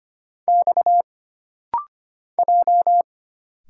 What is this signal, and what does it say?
Morse code, heard as a keyed tone, about 25 wpm, 700 Hz. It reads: X J
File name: Single Character (Letter:Number:Punctuation:Prosign) - Rapid-Fire - 25wpm